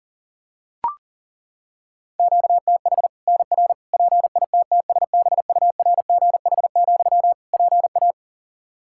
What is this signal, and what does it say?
Morse code, heard as a keyed tone, about 40 wpm, 700 Hz. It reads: QTH NR PITTSBURGH, PA
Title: Practice QSO - Exchange 1 - Beginning - 40wpm